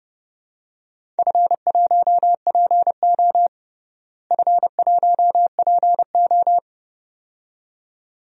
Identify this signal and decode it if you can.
Morse code, heard as a keyed tone, about 31 wpm, 700 Hz. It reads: F1PO F1PO
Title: DXCC-100 Call Signs - 4 Characters - Repeated 2x - 30wpm